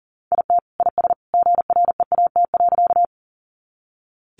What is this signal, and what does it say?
Morse code, heard as a keyed tone, about 40 wpm, 700 Hz. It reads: IT IS GREAT.
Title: Sentences from Top 200 Words 40wpm